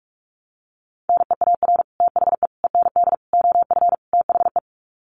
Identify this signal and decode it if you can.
Morse code, heard as a keyed tone, about 44 wpm, 700 Hz. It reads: NEAR THE END OF THE
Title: Sets of 5 Words Encore 45wpm